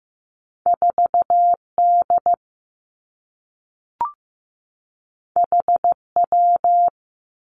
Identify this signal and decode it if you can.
Morse code, heard as a keyed tone, about 15 wpm, 700 Hz. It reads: 4 D H W
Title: Two Challenging Letter-Number Combinations 15wpm